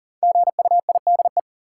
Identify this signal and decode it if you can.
Morse code, heard as a keyed tone, about 40 wpm, 700 Hz. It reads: GUIDE
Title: Sentences from Top 600 Words - New Word Drill - 40wpm